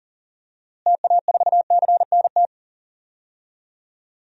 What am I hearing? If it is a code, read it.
Morse code, heard as a keyed tone, about 40 wpm, 700 Hz. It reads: TA4CNT
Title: DXCC-100 Call Signs - 6 Characters - Repeated 2x - 40wpm